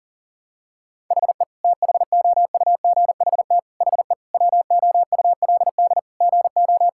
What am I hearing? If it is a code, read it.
Morse code, heard as a keyed tone, about 40 wpm, 700 Hz. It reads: HE THOUGHT HE WOULD GO
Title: Sentences from Top 1000 Words - Review Part 6 - 40wpm